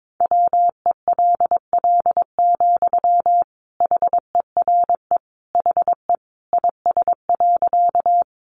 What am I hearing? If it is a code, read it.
Morse code, heard as a keyed tone, about 22 wpm, 700 Hz. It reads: WELL, HERE HE IS.
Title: Sentences from Top 200 Words 22wpm